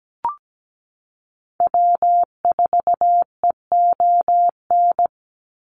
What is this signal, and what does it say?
Morse code, heard as a keyed tone, about 17 wpm, 700 Hz. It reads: W4EON
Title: NA SOTA Activator Callsigns 17wpm